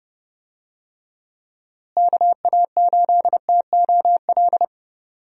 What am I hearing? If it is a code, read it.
Morse code, heard as a keyed tone, about 32 wpm, 700 Hz. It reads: KA8TOL
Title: USA Call Signs - 6 Characters - Repeated 3x - 30wpm